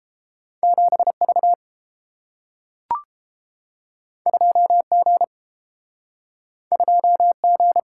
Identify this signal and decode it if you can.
Morse code, heard as a keyed tone, about 31 wpm, 700 Hz. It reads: 7V 2G 2G